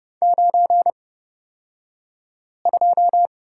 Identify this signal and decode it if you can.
Morse code, heard as a keyed tone, about 30 wpm, spaced as for 5 wpm, 700 Hz. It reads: 92
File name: Lesson 37d - Letter x - Two Characters 30wpm